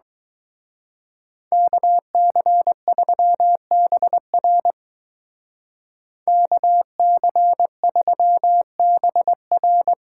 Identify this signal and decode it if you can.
Morse code, heard as a keyed tone, about 20 wpm, 700 Hz. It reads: KC3BR KC3BR
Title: Speed Racing - USA Call Signs 17wpm